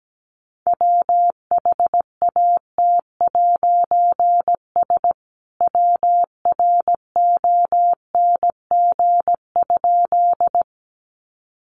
Morse code, heard as a keyed tone, about 17 wpm, 700 Hz. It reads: WHAT'S WRONG?
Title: A Sound Of Thunder 17wpm